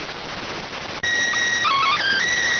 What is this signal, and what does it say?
SSTV (slow-scan television), PD 290.